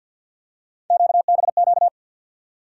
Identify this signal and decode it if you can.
Morse code, heard as a keyed tone, about 53 wpm, 700 Hz. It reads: QBQ
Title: USA Call Sign Postfixes - 3 Letters 50wpm